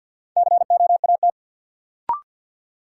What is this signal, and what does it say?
Morse code, heard as a keyed tone, about 53 wpm, 700 Hz. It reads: COAT
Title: Sentences from Top 1600 Words - New Word Drill Alt - 50wpm